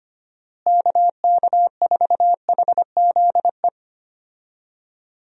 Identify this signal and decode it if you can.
Morse code, heard as a keyed tone, about 25 wpm, 700 Hz. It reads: KK4HZE
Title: USA Call Signs - 6 Characters - Repeated 2x - 25wpm